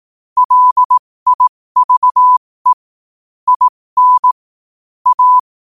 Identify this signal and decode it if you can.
Morse code, heard as a keyed tone, about 18 wpm, spaced as for 15 wpm, 1000 Hz. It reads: LIVE IN A
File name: qotdfiles / 15wpm